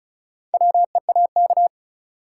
Morse code, heard as a keyed tone, about 34 wpm, 700 Hz. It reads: WEAK
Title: Four Letter Words - 35wpm